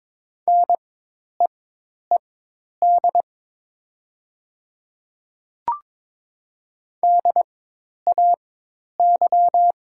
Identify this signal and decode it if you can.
Morse code, heard as a keyed tone, about 23 wpm, spaced as for 10 wpm, 700 Hz. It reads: NEED DAY
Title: 4x Character Spacing - Top 200 Words - 22wpm